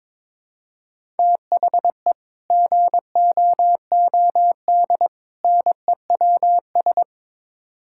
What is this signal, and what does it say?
Morse code, heard as a keyed tone, about 22 wpm, 700 Hz. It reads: THE GOOD NEWS
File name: practice / 22wpm / Build A Phrase - 5-Word Phrases - 22wpm